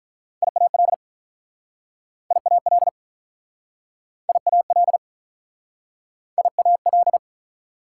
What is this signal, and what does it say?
Morse code, heard as a keyed tone, about 48 wpm, 700 Hz. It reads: IAL IAL IAL IAL